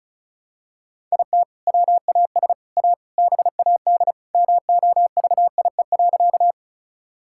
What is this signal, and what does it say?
Morse code, heard as a keyed tone, about 34 wpm, 700 Hz. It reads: IT WAS A BAD MOVIE.
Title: Sentences from Top 1000 Words - Review Part 5 - 35wpm